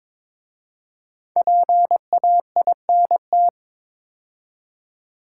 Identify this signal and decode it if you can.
Morse code, heard as a keyed tone, about 22 wpm, 700 Hz. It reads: PAINT
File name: 5 Letter Words - First 3 Letters Given - 22wpm